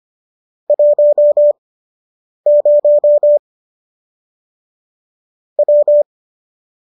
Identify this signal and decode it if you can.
Morse code, heard as a keyed tone, about 25 wpm, spaced as for 8 wpm, 590 Hz. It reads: 10 W